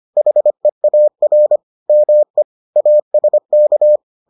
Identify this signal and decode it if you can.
Morse code, heard as a keyed tone, about 25 wpm, 600 Hz. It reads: HEAR ME ASK